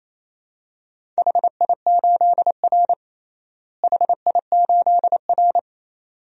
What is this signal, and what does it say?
Morse code, heard as a keyed tone, about 28 wpm, 700 Hz. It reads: HI8R HI8R
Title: DXCC-100 Call Signs - 4 Characters - Repeated 2x - 28wpm